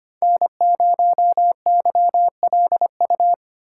Morse code, heard as a keyed tone, about 25 wpm, 700 Hz. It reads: N0YLU
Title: USA Call Signs - 5 Characters - 4x Character Spacing - 25wpm